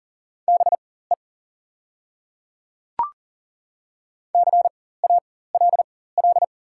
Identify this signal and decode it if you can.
Morse code, heard as a keyed tone, about 38 wpm, spaced as for 18 wpm, 700 Hz. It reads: BE CALL
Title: 4x Character Spacing - Top 100 Words - 40wpm